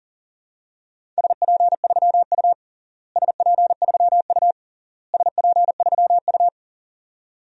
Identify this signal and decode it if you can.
Morse code, heard as a keyed tone, about 37 wpm, 700 Hz. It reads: SP3U SP3U SP3U
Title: DXCC-100 Call Signs - 4 Characters - Repeated 3x - 40wpm